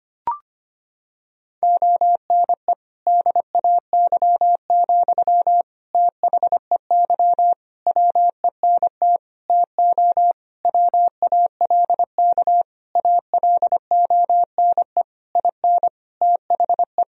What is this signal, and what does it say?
Morse code, heard as a keyed tone, about 25 wpm, 700 Hz. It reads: ONE DAY, THEY WENT TO WALK ALONE IN THE
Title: Sentences from Top 1000 Words - Review Part 2 - 25wpm